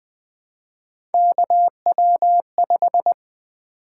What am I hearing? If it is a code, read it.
Morse code, heard as a keyed tone, about 20 wpm, 700 Hz. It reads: KW5
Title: Three Letter-Number Combinations - 20wpm